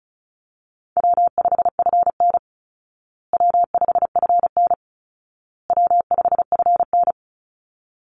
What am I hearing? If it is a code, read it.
Morse code, heard as a keyed tone, about 34 wpm, 700 Hz. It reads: W5FN W5FN W5FN
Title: Easy Call Signs Repeated 3x - 35wpm